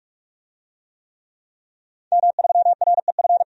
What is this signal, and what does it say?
Morse code, heard as a keyed tone, about 46 wpm, 700 Hz. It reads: M3REF
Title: DXCC-100 Call Signs - 5 Characters - 45wpm